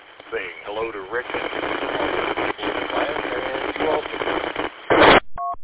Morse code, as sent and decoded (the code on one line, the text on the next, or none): none